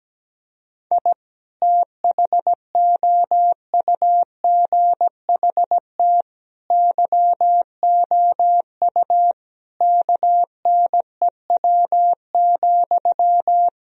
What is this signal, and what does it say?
Morse code, heard as a keyed tone, about 17 wpm, 700 Hz. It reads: I THOUGHT YOU KNEW,
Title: Sentences from Top 1000 Words - Review Part 4 - 17wpm